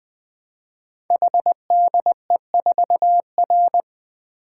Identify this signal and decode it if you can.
Morse code, heard as a keyed tone, about 20 wpm, 700 Hz. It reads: HDE4R
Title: Code Groups - 5 Letter:Numbers - 20wpm